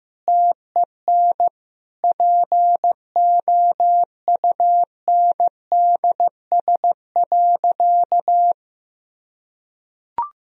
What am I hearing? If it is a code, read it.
Morse code, heard as a keyed tone, about 15 wpm, 700 Hz. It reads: TEN POUNDS.
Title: Sentences from Top 1000 Words - Review Part 4 - 15wpm